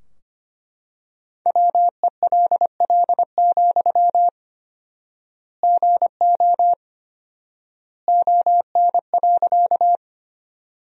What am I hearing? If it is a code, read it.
Morse code, heard as a keyed tone, about 25 wpm, 700 Hz. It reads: WELL, GO ON.
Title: 4x Word-Spacing - Sentences from Top 200 Words - 25wpm